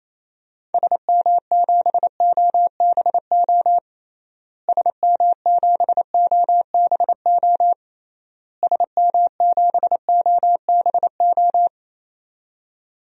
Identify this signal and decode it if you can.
Morse code, heard as a keyed tone, about 28 wpm, 700 Hz. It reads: SM7OBO SM7OBO SM7OBO
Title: DXCC-100 Call Signs - 6 Characters - Repeated 3x - 28wpm